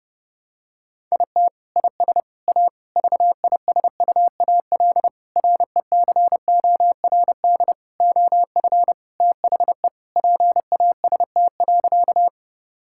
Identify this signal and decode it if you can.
Morse code, heard as a keyed tone, about 30 wpm, 700 Hz. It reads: IT IS A VISUAL RECORD OF THE PAST.
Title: Sentences from Top 1000 Words - Review Part 6 - 30wpm